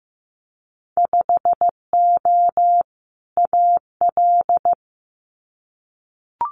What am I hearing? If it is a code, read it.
Morse code, heard as a keyed tone, about 15 wpm, 700 Hz. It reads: 5O AL
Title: Winter Field Day Exchanges 15wpm